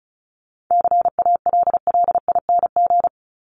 Morse code, heard as a keyed tone, about 34 wpm, 700 Hz. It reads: CALLING
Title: ING Suffix - Top 100 Words with Base Words 35wpm